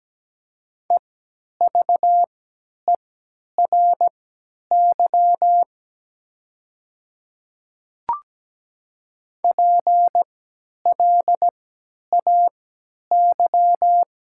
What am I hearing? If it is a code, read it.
Morse code, heard as a keyed tone, about 17 wpm, spaced as for 10 wpm, 700 Hz. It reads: EVERY PLAY
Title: 3x Character Spacing - Top 200 Words - 17wpm